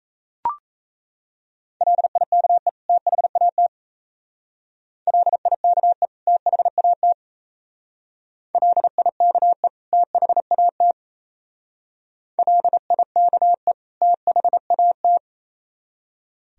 Morse code, heard as a keyed tone, about 34 wpm, 690 Hz. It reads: LIKE THAT LIKE THAT LIKE THAT LIKE THAT
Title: Speed-Racing - Sets of 2 Words 28wpm